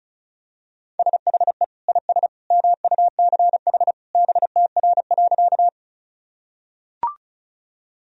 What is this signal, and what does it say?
Morse code, heard as a keyed tone, about 36 wpm, 700 Hz. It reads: SHE IS MUCH BTR.
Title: Sentences from Top 400 Words and Abbreviations 35wpm